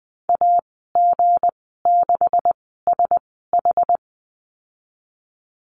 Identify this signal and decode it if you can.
Morse code, heard as a keyed tone, about 20 wpm, spaced as for 15 wpm, 700 Hz. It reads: AG6SH
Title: 167 Lesson 27g - Number 7 - Call Signs 20wpm